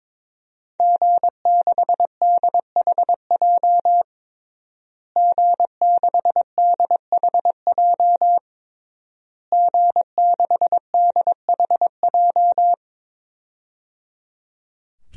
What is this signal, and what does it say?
Morse code, heard as a keyed tone, about 22 wpm, 700 Hz. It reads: G6DHJ G6DHJ G6DHJ
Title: DXCC-100 Call Signs - 5 Characters - Repeated 3x - 22wpm